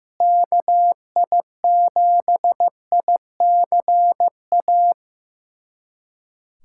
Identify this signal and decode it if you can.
Morse code, heard as a keyed tone, about 15 wpm, 700 Hz. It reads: KI7ICA
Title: USA Callsign DE Callsign - 6 Characters - 15wpm